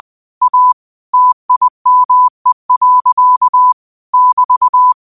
Morse code, heard as a keyed tone, about 20 wpm, 1000 Hz. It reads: A TIME. <BT>